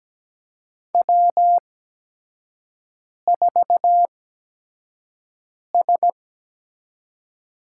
Morse code, heard as a keyed tone, about 17 wpm, spaced as for 5 wpm, 700 Hz. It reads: W4S
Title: USA Call Signs - 8x Character Spacing - 17wpm